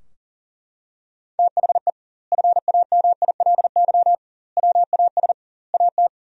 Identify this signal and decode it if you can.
Morse code, heard as a keyed tone, about 40 wpm, 700 Hz. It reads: THE FAMILY WAS AT
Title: 2x Word-Spacing - Sentences from Top 400 Words - 40wpm